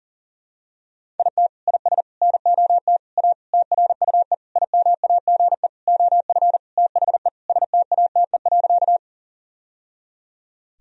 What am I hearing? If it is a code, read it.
Morse code, heard as a keyed tone, about 37 wpm, 700 Hz. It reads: IT IS NOT A TRUE IMAGE OF THE STATE.
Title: Sentences from Top 1000 Words - Review Part 4 - 40wpm